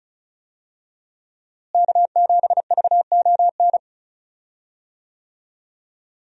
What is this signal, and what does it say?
Morse code, heard as a keyed tone, about 36 wpm, 700 Hz. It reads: K7VON
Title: USA Call Signs - 5 Characters - Repeated 2x - 35wpm